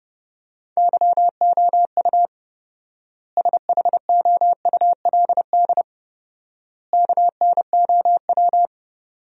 Morse code, heard as a keyed tone, about 30 wpm, 700 Hz. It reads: YOU SHOULD KNOW
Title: 4x Word-Spacing - Sentences from Top 400 Words - 30wpm